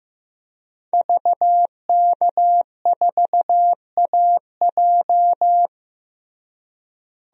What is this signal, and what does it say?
Morse code, heard as a keyed tone, about 15 wpm, 700 Hz. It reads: VK4AJ
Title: DXCC-100 Call Signs - 5 Characters - Repeated 2x - 15wpm